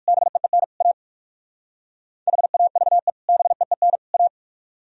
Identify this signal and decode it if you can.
Morse code, heard as a keyed tone, about 66 wpm, spaced as for 56 wpm, 700 Hz. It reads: BEEN A HAVE BEEN A
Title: Speed-Racing - Sets of 3 Words 45wpm